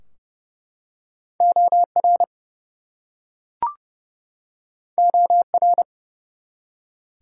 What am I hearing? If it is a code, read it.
Morse code, heard as a keyed tone, about 31 wpm, 700 Hz. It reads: OR OR
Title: Two Letter Words - 30wpm